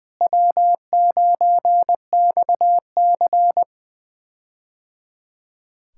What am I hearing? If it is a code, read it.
Morse code, heard as a keyed tone, about 20 wpm, 700 Hz. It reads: W9XC
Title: K1USN SST QSO Practice - 20wpm